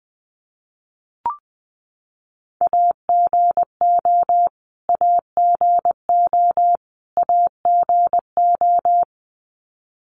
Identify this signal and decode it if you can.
Morse code, heard as a keyed tone, about 20 wpm, 700 Hz. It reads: AGO AGO AGO
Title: Top 500 Words Repeated 20wpm